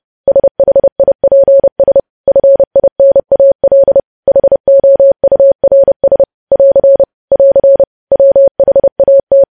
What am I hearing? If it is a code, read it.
Morse code, heard as a keyed tone, about 30 wpm, 550 Hz. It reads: SHIPS FINAL HOURS <AR> <AR> WHAT
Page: news headlines in morse code at 30 wpm